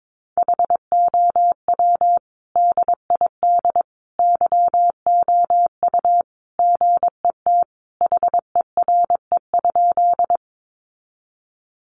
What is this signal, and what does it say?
Morse code, heard as a keyed tone, about 22 wpm, 700 Hz. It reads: HOW DID YOU GET HERE?
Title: Sentences from Top 200 Words 22wpm